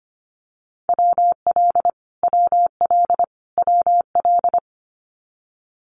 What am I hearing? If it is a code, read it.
Morse code, heard as a keyed tone, about 25 wpm, 700 Hz. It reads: WL WL WL
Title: Abbreviations Repeated 3x - 25wpm